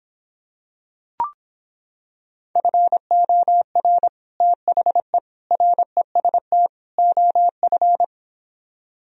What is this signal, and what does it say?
Morse code, heard as a keyed tone, about 26 wpm, 700 Hz. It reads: FOR THE REST OF